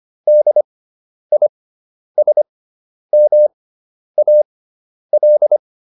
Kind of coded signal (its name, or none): Morse code